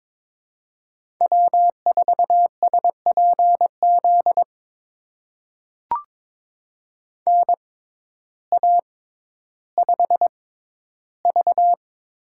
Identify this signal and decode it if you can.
Morse code, heard as a keyed tone, about 22 wpm, spaced as for 12 wpm, 700 Hz. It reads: W4SPZ NA5V
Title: USA Call Signs - 6x Character Spacing - 22wpm